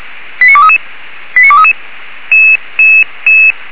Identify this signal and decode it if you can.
Five-tone selective call (ZVEI), heard as quick runs of five tones, tones of about 70 ms. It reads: 89233 89233
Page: /464/SDR/ZVEI/